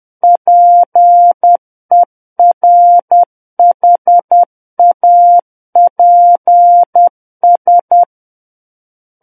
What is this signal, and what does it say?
Morse code, heard as a keyed tone, about 10 wpm, 700 Hz. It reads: PERHAPS
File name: Book One / 01-Chapter 1 - 10 WPM